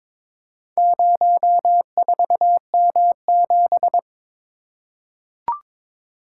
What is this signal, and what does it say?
Morse code, heard as a keyed tone, about 22 wpm, 700 Hz. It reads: 04M7
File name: practice / 22wpm / Four Letter-Number Combinations - 22wpm